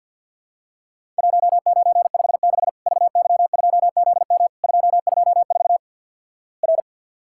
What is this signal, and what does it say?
Morse code, heard as a keyed tone, about 50 wpm, 700 Hz. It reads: 1956 VY2ZM 224 R
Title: Worked All Europe DX Contest - QTC Exchange - 50wpm